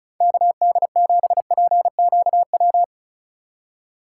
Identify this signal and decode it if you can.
Morse code, heard as a keyed tone, about 36 wpm, 700 Hz. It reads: KD7PQW
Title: USA Call Signs - 6 Characters - 4x Character Spacing - 35wpm